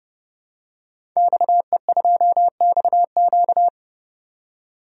Morse code, heard as a keyed tone, about 30 wpm, 700 Hz. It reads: XE2XQ
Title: DXCC-100 Call Signs - 5 Characters - 30wpm